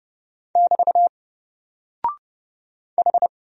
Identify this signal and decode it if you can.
Morse code, heard as a keyed tone, about 30 wpm, 700 Hz. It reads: <BT> H
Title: Single Character (Letter:Number:Punctuation:Prosign) Alt 30wpm